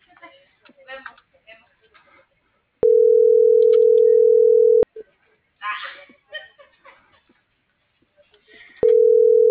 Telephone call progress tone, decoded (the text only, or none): ringback tone